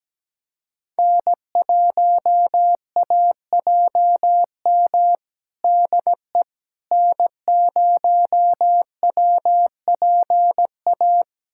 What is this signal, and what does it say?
Morse code, heard as a keyed tone, about 17 wpm, 700 Hz. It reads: N1AJM DE N0WPA
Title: USA Callsign DE Callsign 17wpm